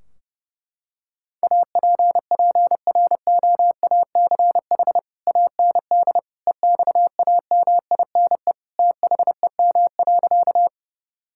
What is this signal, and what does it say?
Morse code, heard as a keyed tone, about 30 wpm, 700 Hz. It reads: APPROACH AND EXAMINE THEM.
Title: Sentences from Top 1600 Words 30wpm